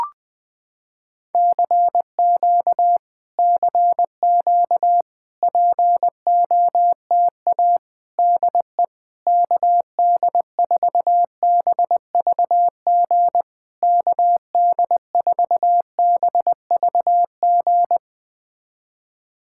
Morse code, heard as a keyed tone, about 20 wpm, 700 Hz. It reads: CQ CQ POTA DE KD4BVG KD4BVG
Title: POTA QSO Practice - 20wpm